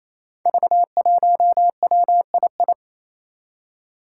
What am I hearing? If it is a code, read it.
Morse code, heard as a keyed tone, about 28 wpm, 700 Hz. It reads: V1WII
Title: Five Challenging Letter-Number Combinations 28wpm